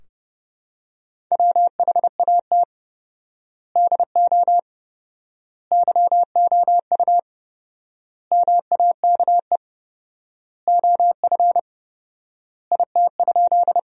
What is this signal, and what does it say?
Morse code, heard as a keyed tone, about 30 wpm, 700 Hz. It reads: WHAT DO YOU MAKE OF IT?
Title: 4x Word-Spacing - Sentences from Top 100 Words - 30wpm